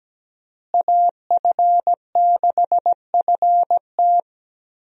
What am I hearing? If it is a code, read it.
Morse code, heard as a keyed tone, about 17 wpm, 700 Hz. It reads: AF6FT
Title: USA Call Signs - 5 Characters - Repeated 3x - 17wpm